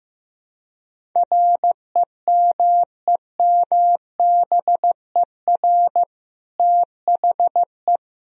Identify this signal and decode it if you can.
Morse code, heard as a keyed tone, about 15 wpm, 700 Hz. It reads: REMEMBER THE
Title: Sentences from Top 1000 Words - Review Part 1 - 15wpm